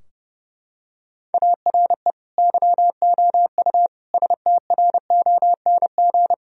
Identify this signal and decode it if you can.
Morse code, heard as a keyed tone, about 30 wpm, 700 Hz. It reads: ARE YOU STRONG
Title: Sentences from Top 1000 Words - Review Part 3 - 30wpm